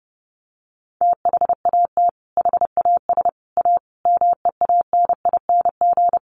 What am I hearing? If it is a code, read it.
Morse code, heard as a keyed tone, about 30 wpm, 700 Hz. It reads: THAT HAS A MEANING
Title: Sentences from Top 900 Words 30wpm